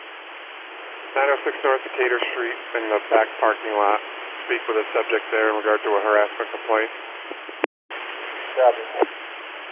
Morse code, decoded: none

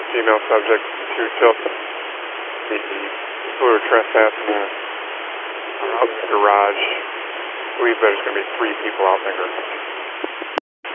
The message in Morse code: none

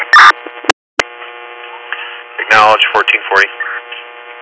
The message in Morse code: none

